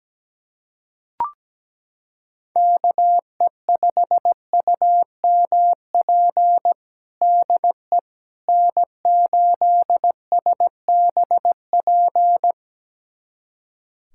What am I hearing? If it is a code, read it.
Morse code, heard as a keyed tone, about 17 wpm, 700 Hz. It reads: KE5UMP DE N8SBP
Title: USA Callsign DE Callsign 17wpm